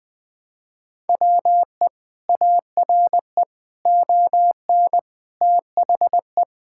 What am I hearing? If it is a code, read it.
Morse code, heard as a keyed tone, about 21 wpm, 700 Hz. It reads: WE ARE ON THE
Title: Sentences with 4-Word Phrases - Spoken Beginning - 20wpm